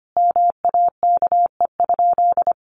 Morse code, heard as a keyed tone, about 25 wpm, 700 Hz. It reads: MAKE?
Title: Sentences from Top 1400 Words 25wpm